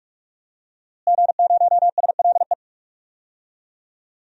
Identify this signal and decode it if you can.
Morse code, heard as a keyed tone, about 47 wpm, 700 Hz. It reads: G0SLE
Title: DXCC-100 Call Signs - 5 Characters - Repeated 2x - 45wpm